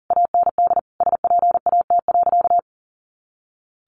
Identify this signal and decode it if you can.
Morse code, heard as a keyed tone, about 40 wpm, 700 Hz. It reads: AND SPAT.
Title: A Sound Of Thunder 40wpm